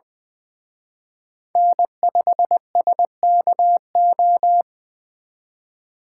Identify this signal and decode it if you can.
Morse code, heard as a keyed tone, about 23 wpm, 700 Hz. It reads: N5SKO
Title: Speed Racing - USA 1x3 Call Signs 20wpm